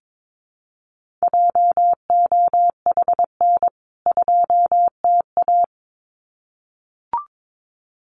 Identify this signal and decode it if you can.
Morse code, heard as a keyed tone, about 22 wpm, 700 Hz. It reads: JOHN 2TA
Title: CWT Exchange 22wpm